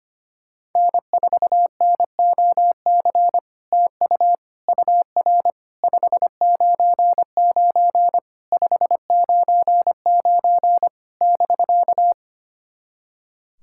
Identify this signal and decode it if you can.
Morse code, heard as a keyed tone, about 25 wpm, 700 Hz. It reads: N4NOC TU UR 599 599 <BK>
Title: SOTA QSO Practice 25wpm